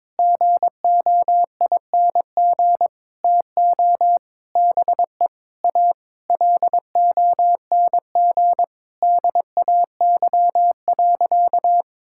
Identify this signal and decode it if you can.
Morse code, heard as a keyed tone, about 22 wpm, 700 Hz. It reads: GOING TO BE A LONG DAY.
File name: Sentences with 7-Word Phrases - 22wpm